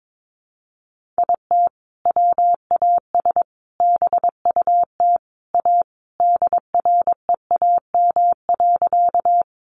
Morse code, heard as a keyed tone, about 22 wpm, 700 Hz. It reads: IT WAS BUT A DREAM.
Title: Sentences from Top 1500 Words 22wpm